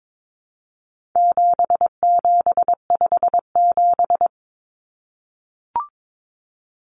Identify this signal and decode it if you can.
Morse code, heard as a keyed tone, about 22 wpm, 700 Hz. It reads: 7757
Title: Numbers - 4-Digit 22wpm